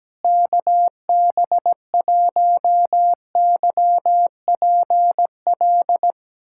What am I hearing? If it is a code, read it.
Morse code, heard as a keyed tone, about 17 wpm, 700 Hz. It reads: KB1YPL
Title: USA Callsign DE Callsign - 6 Characters - 17wpm